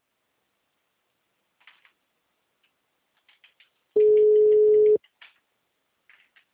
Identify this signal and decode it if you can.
Ringback tone (Europe, 425 Hz).